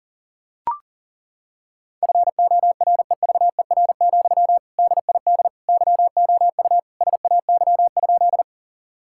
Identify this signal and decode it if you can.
Morse code, heard as a keyed tone, about 37 wpm, 700 Hz. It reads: FOREVER, DID YOU SAY?
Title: Sentences from Top 1600 Words 40wpm